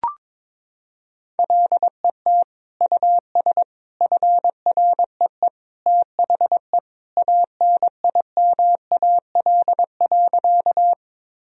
Morse code, heard as a keyed tone, about 21 wpm, 700 Hz. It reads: LET US FREE THE ANIMAL.
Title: Sentences from Top 1000 Words - Review Part 1 - 22wpm